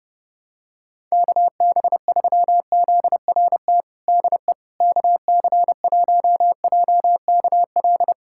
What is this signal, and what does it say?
Morse code, heard as a keyed tone, about 30 wpm, 700 Hz. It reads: KB3ZRT DE KC1JKL
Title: USA Callsign DE Callsign 30wpm